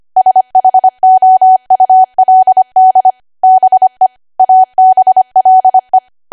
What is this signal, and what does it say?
Morse code, heard as a keyed tone, about 25 wpm, 750 Hz. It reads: SHOULD BE ABLE
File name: Archive / 25 WPM / 25WPM